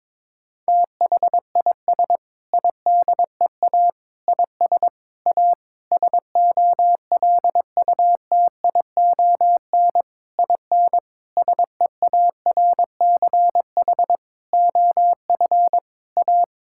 Morse code, heard as a keyed tone, about 21 wpm, 700 Hz. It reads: THIS IDEA IS A SOLUTION IN SEARCH OF A
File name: practice / 22wpm / Sentences with 8-Word Phrases - 22wpm